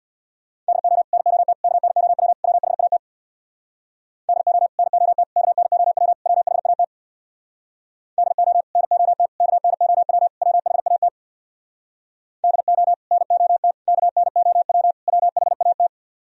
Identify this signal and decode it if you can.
Morse code, heard as a keyed tone, about 62 wpm, 700 Hz. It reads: DO NOT KNOW WHAT DO NOT KNOW WHAT DO NOT KNOW WHAT DO NOT KNOW WHAT
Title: Speed-Racing - Sets of 4 Words 50wpm